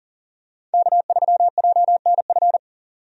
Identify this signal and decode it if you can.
Morse code, heard as a keyed tone, about 40 wpm, 700 Hz. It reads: K3JNF
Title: USA Call Signs - 5 Characters - 40wpm